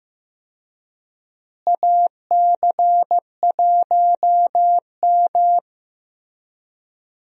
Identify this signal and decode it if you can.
Morse code, heard as a keyed tone, about 15 wpm, 700 Hz. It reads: AC1M
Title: USA Call Signs - 4 Characters - Repeated 3x - 15wpm